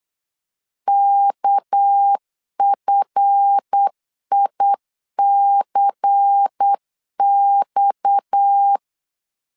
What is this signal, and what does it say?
Morse code, heard as a keyed tone, about 8 wpm, 800 Hz. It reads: KFICX
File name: LEZIONI / 7 GF9Z / 7 GF 8wpm 3108